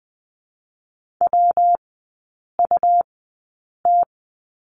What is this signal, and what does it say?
Morse code, heard as a keyed tone, about 20 wpm, 700 Hz. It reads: W U T
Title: 114 Lesson 19e - Letter w - Three Characters 20wpm